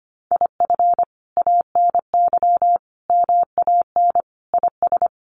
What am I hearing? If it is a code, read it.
Morse code, heard as a keyed tone, about 25 wpm, 700 Hz. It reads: IF ANY MAN IS
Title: Sentences from Top 1400 Words 25wpm